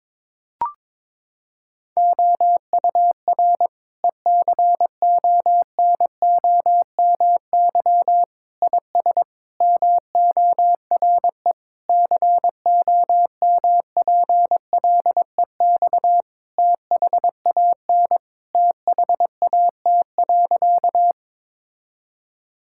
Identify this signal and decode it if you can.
Morse code, heard as a keyed tone, about 21 wpm, 700 Hz. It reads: OUR ECONOMY IS MORE COMPLEX THAN THAT.
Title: Sentences from Top 1000 Words - Review Part 6 - 22wpm